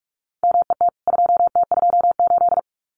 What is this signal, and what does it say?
Morse code, heard as a keyed tone, about 44 wpm, 700 Hz. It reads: MET 2T28
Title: CWT Exchange 45wpm